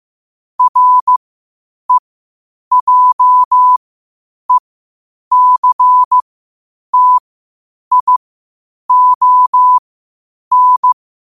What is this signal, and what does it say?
Morse code, heard as a keyed tone, about 15 wpm, spaced as for 8 wpm, 1000 Hz. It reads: REJECTION
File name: qotdfiles / 8wpm